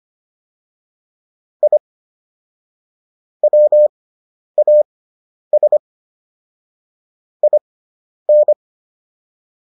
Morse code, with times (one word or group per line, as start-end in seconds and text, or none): 1.6-1.8 I
3.4-5.8 WAS
7.4-8.5 IN